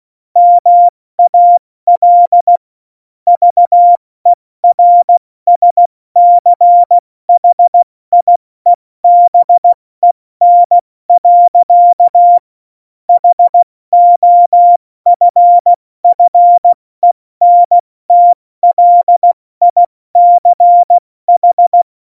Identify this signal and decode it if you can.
Morse code, heard as a keyed tone, about 16 wpm, 700 Hz. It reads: MAL VERSCHIEBEN. HOFFENTLICH
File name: Morsekurs / Witze / Witz 13. August 2019, 70 F80